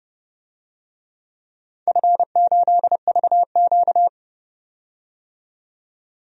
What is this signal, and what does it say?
Morse code, heard as a keyed tone, about 30 wpm, 700 Hz. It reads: F8VQ
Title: DXCC-100 Call Signs - 4 Characters - 30wpm